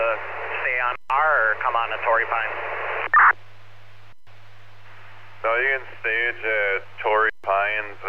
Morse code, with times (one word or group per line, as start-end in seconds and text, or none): none